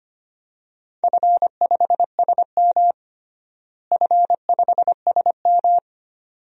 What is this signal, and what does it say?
Morse code, heard as a keyed tone, about 25 wpm, 700 Hz. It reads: F5SM F5SM
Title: DXCC-100 Call Signs - 4 Characters - Repeated 2x - 25wpm